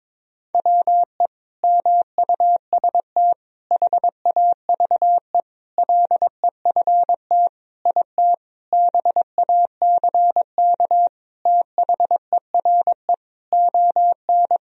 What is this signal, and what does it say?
Morse code, heard as a keyed tone, about 22 wpm, 700 Hz. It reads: WE MUST HAVE LEFT IT BACK THERE ON
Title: Sentences from Top 1000 Words - Review Part 2 - 22wpm